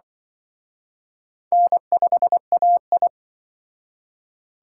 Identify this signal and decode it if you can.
Morse code, heard as a keyed tone, about 24 wpm, 690 Hz. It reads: N5AI
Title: Speed Racing - USA 1x2 Call Signs 20wpm